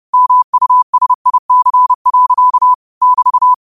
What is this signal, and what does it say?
Morse code, heard as a keyed tone, about 30 wpm, 1000 Hz. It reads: MUSIC. <BT>
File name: qotdfiles / 30wpm